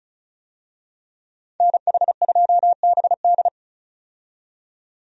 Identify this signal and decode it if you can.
Morse code, heard as a keyed tone, about 34 wpm, 700 Hz. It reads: NH2BD
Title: USA Call Signs - 5 Characters - 35wpm